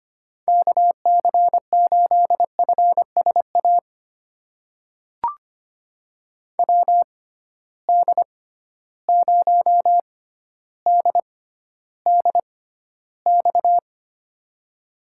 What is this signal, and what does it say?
Morse code, heard as a keyed tone, about 25 wpm, spaced as for 13 wpm, 700 Hz. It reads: KC8FSA WD0DDX
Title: USA Call Signs - 6 Characters - 6x Character Spacing - 25wpm